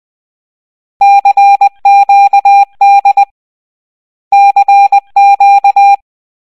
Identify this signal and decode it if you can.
Morse code, heard as a keyed tone, about 20 wpm, 800 Hz. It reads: CQD CQ
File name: raving-rabbids-travel-in-time-wii-gamerip-2010 / aqecklreoi / 10-05. Morse code